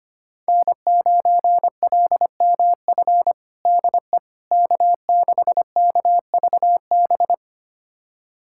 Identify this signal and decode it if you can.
Morse code, heard as a keyed tone, about 25 wpm, 700 Hz. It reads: N9LMF DE K6KVB
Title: USA Callsign DE Callsign - 5 Characters - 25wpm